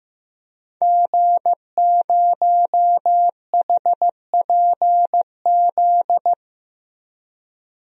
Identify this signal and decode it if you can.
Morse code, heard as a keyed tone, about 15 wpm, 700 Hz. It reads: G0HPZ
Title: DXCC-100 Call Signs - 5 Characters - Repeated 2x - 15wpm